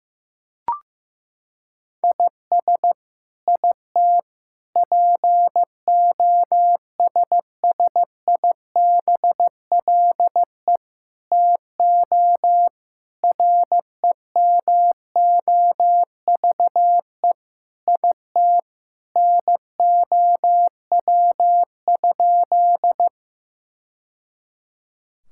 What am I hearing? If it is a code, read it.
Morse code, heard as a keyed tone, about 15 wpm, 700 Hz. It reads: IS IT POSSIBLE TO REMOVE IT NOW?
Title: Sentences from Top 1000 Words - Review Part 5 - 15wpm